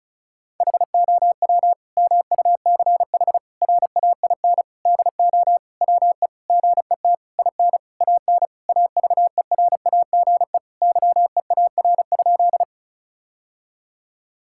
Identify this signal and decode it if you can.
Morse code, heard as a keyed tone, about 36 wpm, 700 Hz. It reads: HOW MUCH RAIN DO WE GET IN AN AVERAGE YEAR?
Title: Sentences from Top 1000 Words - Review Part 6 - 35wpm